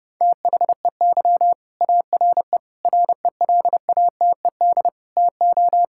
Morse code, heard as a keyed tone, about 31 wpm, 700 Hz. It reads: THEY ARE RELATED TO
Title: Sentences from Top 1000 Words - Review Part 5 - 30wpm